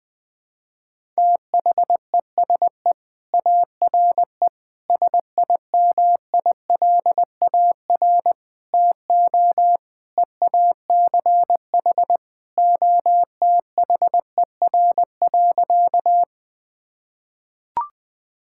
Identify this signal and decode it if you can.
Morse code, heard as a keyed tone, about 21 wpm, 700 Hz. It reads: THESE ARE SIMILAR TO EACH OTHER.
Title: Sentences from Top 1000 Words - Review Part 5 - 20wpm